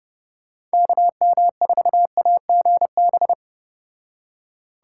Morse code, heard as a keyed tone, about 35 wpm, 700 Hz. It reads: KM4AGB